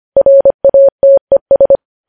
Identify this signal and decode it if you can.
Morse code, heard as a keyed tone, about 25 wpm, 550 Hz. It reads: RATES
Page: news headlines in morse code at 25 wpm